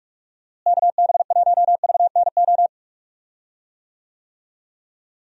Morse code, heard as a keyed tone, about 44 wpm, 700 Hz. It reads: KB1VNO
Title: USA Call Signs - 6 Characters - Repeated 3x - 45wpm